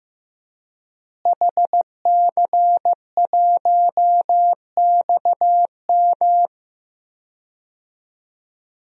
Morse code, heard as a keyed tone, about 15 wpm, 700 Hz. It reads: HC1XM
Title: DXCC-100 Call Signs - 5 Characters - Repeated 3x - 15wpm